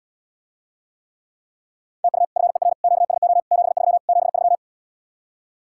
Morse code, heard as a keyed tone, about 80 wpm, 700 Hz. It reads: TU FER QSO 73 73